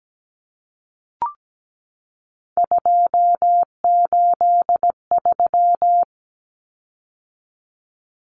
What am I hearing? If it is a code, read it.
Morse code, heard as a keyed tone, about 17 wpm, 700 Hz. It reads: 283
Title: Common Cut Numbers - 3-Digit 17wpm